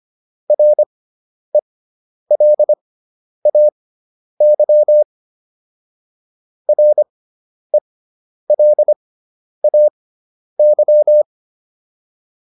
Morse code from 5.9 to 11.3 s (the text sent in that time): RELAY